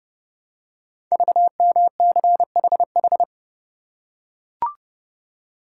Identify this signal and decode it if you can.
Morse code, heard as a keyed tone, about 30 wpm, 700 Hz. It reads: VMCHH
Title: Code Groups - 5 Letter:Numbers - 30wpm